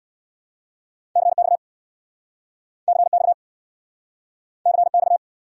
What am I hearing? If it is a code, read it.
Morse code, heard as a keyed tone, about 69 wpm, 700 Hz. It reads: CX CX CX